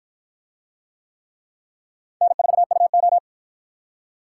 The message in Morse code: N4UO

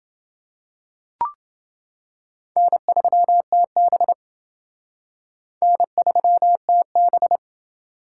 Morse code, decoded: N3TB N3TB

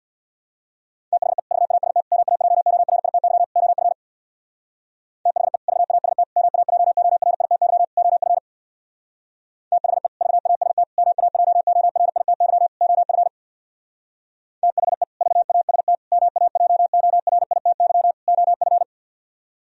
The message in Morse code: THE VAST MAJORITY OF THE VAST MAJORITY OF THE VAST MAJORITY OF THE VAST MAJORITY OF

